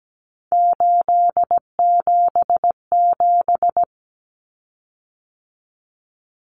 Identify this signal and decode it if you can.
Morse code, heard as a keyed tone, about 17 wpm, 700 Hz. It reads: 877
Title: Common Cut Numbers - 3-Digit 17wpm